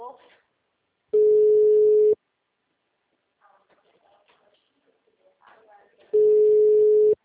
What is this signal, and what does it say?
Ringback tone (Europe, 425 Hz), with speech.